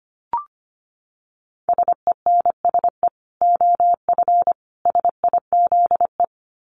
Morse code, heard as a keyed tone, about 25 wpm, 700 Hz. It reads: SENSE OF SIZE
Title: Dit Challenge 25wpm